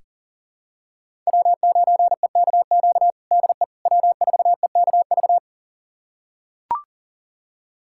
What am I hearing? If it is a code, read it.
Morse code, heard as a keyed tone, about 37 wpm, 700 Hz. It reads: W9EKQ DE W4EKV
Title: USA Callsign DE Callsign - 5 Characters - 40wpm